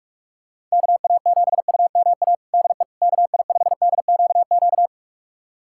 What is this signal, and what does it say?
Morse code, heard as a keyed tone, about 46 wpm, 700 Hz. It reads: KA7UMA DE KI5DQQ
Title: USA Callsign DE Callsign - 6 Characters - 45wpm